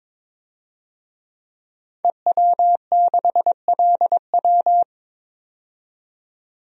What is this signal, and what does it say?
Morse code, heard as a keyed tone, about 22 wpm, 700 Hz. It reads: EW6LW
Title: DXCC-100 Call Signs - 5 Characters - 22wpm